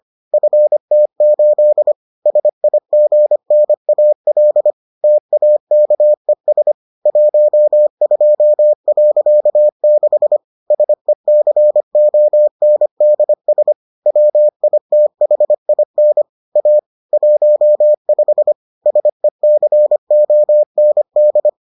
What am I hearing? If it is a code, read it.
Morse code, heard as a keyed tone, about 25 wpm, 600 Hz. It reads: FT8 SIGNAL TAKES 12.6 SECONDS WITHIN A 15 SECOND